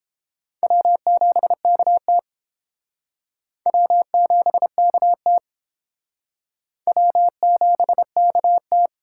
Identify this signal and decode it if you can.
Morse code, heard as a keyed tone, about 31 wpm, 700 Hz. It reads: W7KT W7KT W7KT